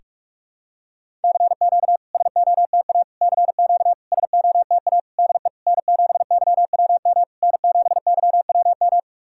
Morse code, heard as a keyed tone, about 46 wpm, 690 Hz. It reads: CQ SOTA CQ SOTA DE N7YWM N7YWM
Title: SOTA QSO Practice 45wpm